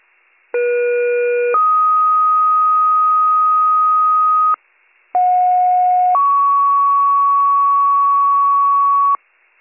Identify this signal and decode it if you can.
Two-tone page, heard as a one-second tone, then a three-second tone, about 480 then 1250 Hz, about 730 then 1120 Hz. It is heard from 0.5 to 9.2 s.